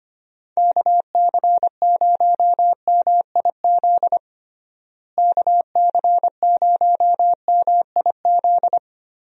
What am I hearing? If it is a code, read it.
Morse code, heard as a keyed tone, about 25 wpm, 700 Hz. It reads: KC0MIZ KC0MIZ
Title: USA Call Signs - 6 Characters - Repeated 2x - 25wpm